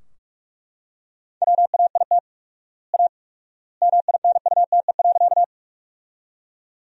Morse code, heard as a keyed tone, about 41 wpm, 700 Hz. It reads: WAIT A MINUTE.
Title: 4x Word-Spacing - Sentences from Top 500 Words - 45wpm